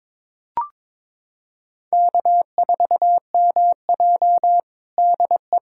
Morse code, heard as a keyed tone, about 22 wpm, 700 Hz. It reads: K4MJ DE
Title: USA Callsign DE Callsign - 4 Characters - 22wpm